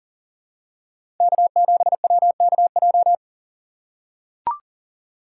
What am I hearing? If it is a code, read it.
Morse code, heard as a keyed tone, about 40 wpm, 700 Hz. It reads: K7WKJ
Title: USA Call Signs - 5 Characters - 2x Character Spacing - 40wpm